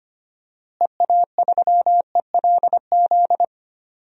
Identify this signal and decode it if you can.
Morse code, heard as a keyed tone, about 25 wpm, 700 Hz. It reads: EA3ELZ
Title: DXCC-100 Call Signs - 6 Characters - Repeated 2x - 25wpm